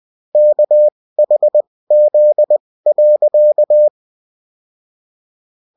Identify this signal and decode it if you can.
Morse code, heard as a keyed tone, about 20 wpm, spaced as for 16 wpm, 600 Hz. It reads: KHZ.